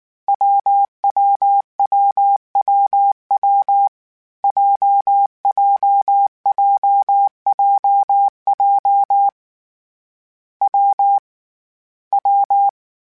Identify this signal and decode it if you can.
Morse code, heard as a keyed tone, about 19 wpm, 800 Hz. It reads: WWWWW JJJJJ W W
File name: Variable Speed / Variable Speed-mp3s / WPJ WPJ1 sp @18